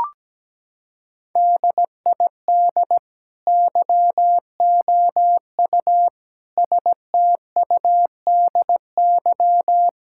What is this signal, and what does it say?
Morse code, heard as a keyed tone, about 17 wpm, 700 Hz. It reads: DID YOU STUDY
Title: Sentences from Top 1000 Words - Review Part 6 - 17wpm